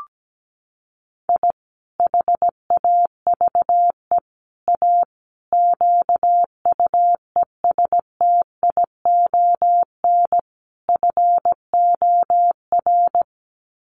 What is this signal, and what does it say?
Morse code, heard as a keyed tone, about 17 wpm, 700 Hz. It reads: I HAVE A QUESTION FOR
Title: Sets of 5 Words Encore2 17wpm